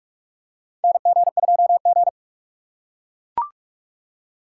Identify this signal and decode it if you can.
Morse code, heard as a keyed tone, about 44 wpm, 700 Hz. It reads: NG2G
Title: USA Call Signs - 4x Character Spacing - 45wpm